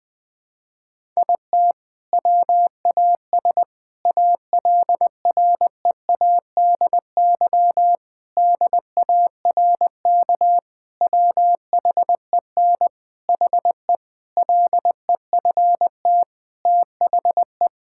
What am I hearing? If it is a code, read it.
Morse code, heard as a keyed tone, about 21 wpm, 700 Hz. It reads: IT WAS ALREADY DARK WHEN HE LEFT THE
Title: Sentences from Top 1000 Words - Review Part 5 - 20wpm